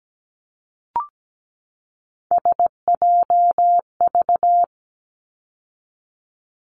Morse code, heard as a keyed tone, about 17 wpm, 700 Hz. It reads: SJV
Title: ARRL Sections 17wpm